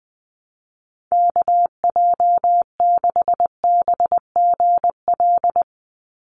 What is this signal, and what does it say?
Morse code, heard as a keyed tone, about 20 wpm, 700 Hz. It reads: KJ6BGL
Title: USA Call Signs 20wpm